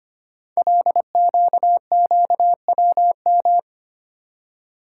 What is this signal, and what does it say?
Morse code, heard as a keyed tone, about 25 wpm, 700 Hz. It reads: LQQWM
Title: Code Groups - 5 Letters - 25wpm